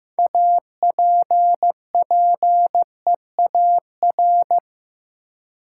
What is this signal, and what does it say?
Morse code, heard as a keyed tone, about 15 wpm, 700 Hz. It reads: APPEAR
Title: Sentences from Top 1200 Words - New Word Drill Alt - 15wpm